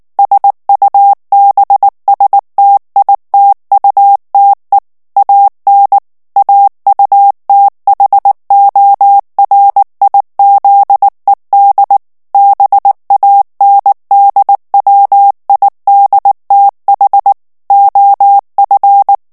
Morse code, heard as a keyed tone, about 19 wpm, 800 Hz. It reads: SUBSTITUTE AN AUTHORIZED BANDWIDTH OF
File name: ANGLAIS texte clair 3 vitesse 18 mots minute